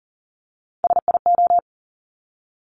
Morse code, heard as a keyed tone, about 38 wpm, 700 Hz. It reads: SIO
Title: Top Three Letter Combinations in Words 40wpm